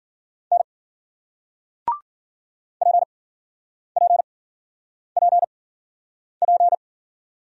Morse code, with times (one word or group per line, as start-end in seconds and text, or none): 0.5-0.6 N
2.8-6.7 PPPP